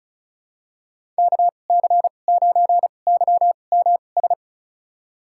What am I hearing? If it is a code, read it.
Morse code, heard as a keyed tone, about 36 wpm, spaced as for 25 wpm, 700 Hz. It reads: KC9YMS
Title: USA Call Signs - 2x Character Spacing - 35wpm